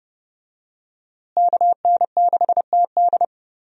Morse code, heard as a keyed tone, about 30 wpm, 700 Hz. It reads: KN6TD
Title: USA Call Signs - 5 Characters - Repeated 2x - 30wpm